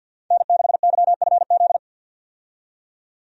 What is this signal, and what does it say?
Morse code, heard as a keyed tone, about 51 wpm, 700 Hz. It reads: N6YFZ
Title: USA Call Signs - 5 Characters - 50wpm